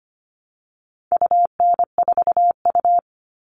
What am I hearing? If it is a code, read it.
Morse code, heard as a keyed tone, about 25 wpm, 700 Hz. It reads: UN4U
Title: All Cut Numbers - 4-Digit 25wpm